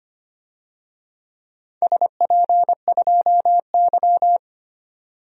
Morse code, heard as a keyed tone, about 25 wpm, 700 Hz. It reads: SP2Y
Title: DXCC-100 Call Signs - 4 Characters - 25wpm